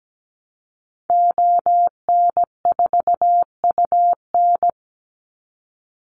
Morse code, heard as a keyed tone, about 17 wpm, 700 Hz. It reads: ON4UN
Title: Call Signs Repeated 2x - 17wpm